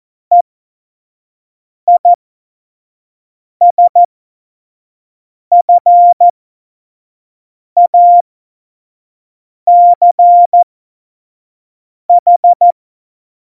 Morse code, heard as a keyed tone, about 14 wpm, spaced as for 5 wpm, 700 Hz. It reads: EISFACH
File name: Morsekurs / Witze / Witz 29. August 17, Tempo 25 F70